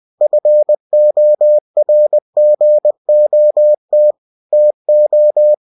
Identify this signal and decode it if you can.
Morse code, heard as a keyed tone, about 20 wpm, 600 Hz. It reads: FORGOT TO
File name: cw-books / winnie-the-pooh / winnie-20wpm-page-0024